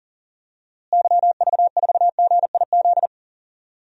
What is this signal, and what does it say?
Morse code, heard as a keyed tone, about 40 wpm, 700 Hz. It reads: YV4GIZ
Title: DXCC-100 Call Signs - 6 Characters - Repeated 2x - 40wpm